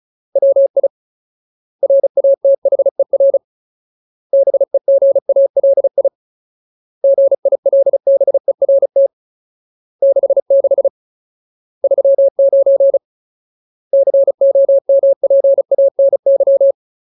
Morse code, heard as a keyed tone, about 34 wpm, 550 Hz. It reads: WI RATHER BEGALI GILBERT 66 39 COMPANY
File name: Ultimate Word List / 35WPM Ultimate